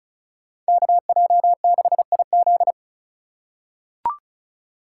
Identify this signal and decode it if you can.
Morse code, heard as a keyed tone, about 36 wpm, 690 Hz. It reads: KJ6IZ
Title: USA Call Signs - 3x Character Spacing - 35wpm